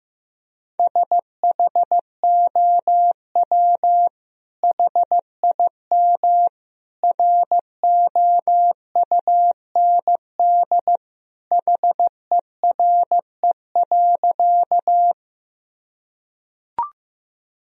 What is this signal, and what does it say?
Morse code, heard as a keyed tone, about 15 wpm, 700 Hz. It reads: SHOW HIM ROUND HERE.
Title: Sentences from Top 1000 Words - Review Part 1 - 15wpm